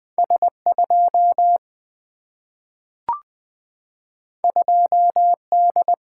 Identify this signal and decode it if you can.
Morse code, heard as a keyed tone, about 20 wpm, 700 Hz. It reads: S2 2D
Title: Two Letter-Number Combinations - 20wpm